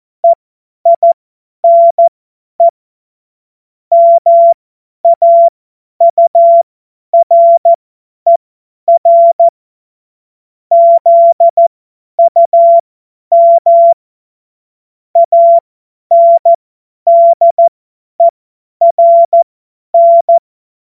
Morse code, heard as a keyed tone, about 14 wpm, spaced as for 10 wpm, 680 Hz. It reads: EINE MAURER ZUM ANDERN
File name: Morsekurs / Witze / Witz 31.Oktober 2023 50 F70